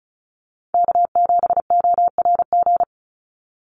Morse code, heard as a keyed tone, about 34 wpm, 700 Hz. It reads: K7ORG
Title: NA SOTA Chaser Callsigns 35wpm